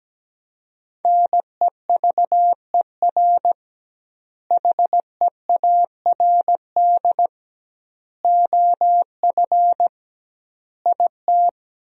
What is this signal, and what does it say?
Morse code, heard as a keyed tone, about 17 wpm, 700 Hz. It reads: NEVER HEARD OF IT
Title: 2x Word-Spacing - Sentences from Top 400 Words - 17wpm